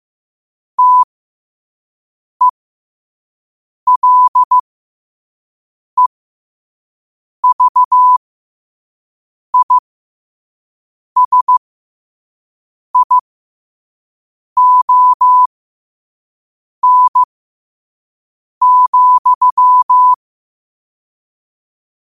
Morse code, heard as a keyed tone, about 15 wpm, spaced as for 5 wpm, 1000 Hz. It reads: TELEVISION,
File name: 5wpm